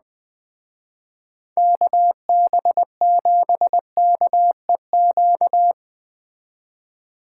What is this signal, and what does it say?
Morse code, heard as a keyed tone, about 21 wpm, 700 Hz. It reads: KB7KEQ
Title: Speed Racing - USA 2x3 Call Signs 17wpm